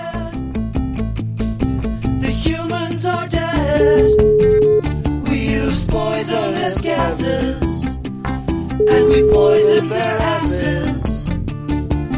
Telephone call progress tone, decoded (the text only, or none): ringback tone